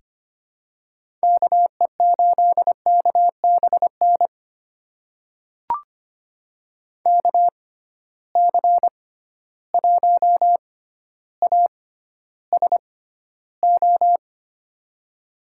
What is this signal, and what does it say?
Morse code, heard as a keyed tone, about 25 wpm, spaced as for 13 wpm, 700 Hz. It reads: KE8KBN KC1ASO
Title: USA Call Signs - 6x Character Spacing - 25wpm